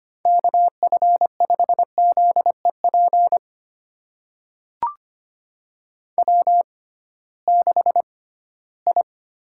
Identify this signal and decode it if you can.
Morse code, heard as a keyed tone, about 25 wpm, spaced as for 16 wpm, 700 Hz. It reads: KF5ZEP W6I
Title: USA Call Signs - 6x Character Spacing - 25wpm